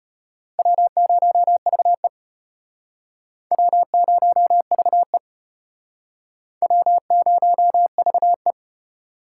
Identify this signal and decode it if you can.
Morse code, heard as a keyed tone, about 31 wpm, 700 Hz. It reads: W0VE W0VE W0VE